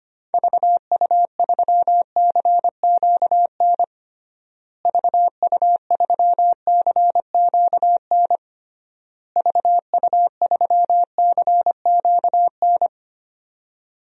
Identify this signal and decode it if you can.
Morse code, heard as a keyed tone, about 25 wpm, 690 Hz. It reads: VU3CQN VU3CQN VU3CQN
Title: DXCC-100 Call Signs - 6 Characters - Repeated 3x - 25wpm